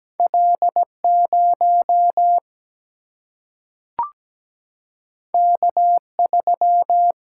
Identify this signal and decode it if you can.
Morse code, heard as a keyed tone, about 17 wpm, 700 Hz. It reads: L0 K3
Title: Single-Letter Single-Number 17wpm